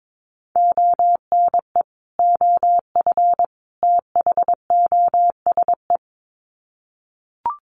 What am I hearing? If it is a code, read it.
Morse code, heard as a keyed tone, about 22 wpm, 700 Hz. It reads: ONE OF THOSE
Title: Sets of 3 Words 22wpm